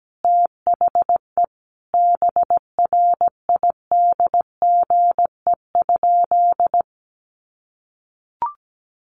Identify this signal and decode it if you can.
Morse code, heard as a keyed tone, about 17 wpm, 700 Hz. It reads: THE BRIDGE?
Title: Sentences from Top 1200 Words 17wpm